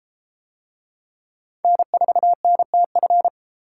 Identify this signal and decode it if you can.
Morse code, heard as a keyed tone, about 33 wpm, 700 Hz. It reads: N4NTF